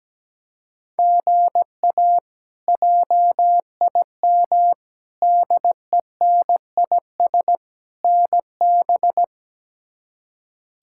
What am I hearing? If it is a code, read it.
Morse code, heard as a keyed tone, about 17 wpm, 700 Hz. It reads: GA JIM DENIS NB
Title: K1USN SST QSO Practice - 17wpm